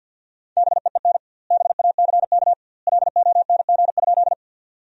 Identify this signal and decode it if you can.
Morse code, heard as a keyed tone, about 50 wpm, 700 Hz. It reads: BEEN BACK LONG?
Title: 2x Word-Spacing - Sentences from Top 200 Words - 50wpm